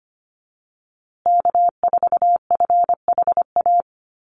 Morse code, heard as a keyed tone, about 25 wpm, 700 Hz. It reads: K4FHA
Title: USA Call Signs Repeated 2x 25wpm